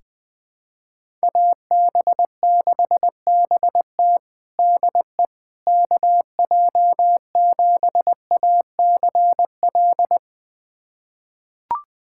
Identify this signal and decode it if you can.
Morse code, heard as a keyed tone, about 21 wpm, 700 Hz. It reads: AB6BT DE KJ7ACL
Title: USA Callsign DE Callsign 20wpm